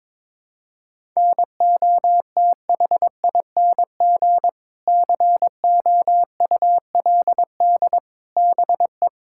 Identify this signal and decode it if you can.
Morse code, heard as a keyed tone, about 22 wpm, 700 Hz. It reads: NOTHING COULD BE
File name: Sentences with 7-Word Phrases - 22wpm